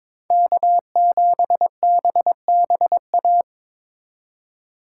Morse code, heard as a keyed tone, about 26 wpm, 700 Hz. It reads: K7BBA
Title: Speed Racing - USA 1x3 Call Signs 22wpm